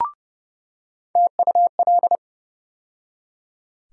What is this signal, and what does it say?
Morse code, heard as a keyed tone, about 30 wpm, 700 Hz. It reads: TUL
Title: USA Call Sign Postfixes - 3 Letters 30wpm